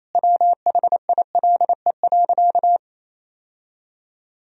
Morse code, heard as a keyed tone, about 28 wpm, 700 Hz. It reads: WHILE.
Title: Sentences from Top 1000 Words - Review Part 1 - 28wpm